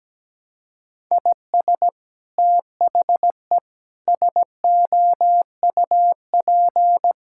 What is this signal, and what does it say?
Morse code, heard as a keyed tone, about 17 wpm, 700 Hz. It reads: IS THE SOUP
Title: Sentences from Top 1000 Words - Review Part 6 - 17wpm